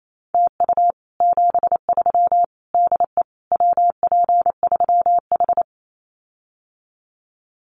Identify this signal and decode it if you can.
Morse code, heard as a keyed tone, about 28 wpm, 700 Hz. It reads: TU 73 DE WP3H
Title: Common QSO Phrases Encore 28wpm